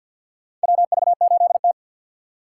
Morse code, heard as a keyed tone, about 52 wpm, 700 Hz. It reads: WV8T
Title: USA Call Signs - 4 Characters - 3x Character Spacing - 50wpm